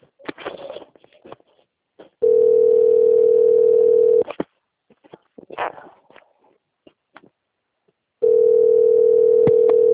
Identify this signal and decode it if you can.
Ringback tone (North America), 440 + 480 Hz.
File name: out- - - - - .38988